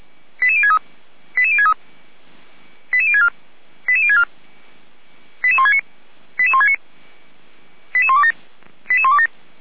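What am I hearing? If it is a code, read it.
Five-tone selective call (ZVEI), heard as quick runs of five tones, tones of about 70 ms. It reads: 88063 88063 88064 88064 88169 88169 88127 88127
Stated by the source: Five-tone ZVEI selective call 88199 received on SDR, from the server of feuerwehr-orlamuende (fire brigade Orlamünde).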